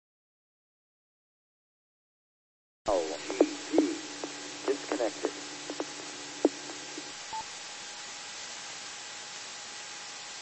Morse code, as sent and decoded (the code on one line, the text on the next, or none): none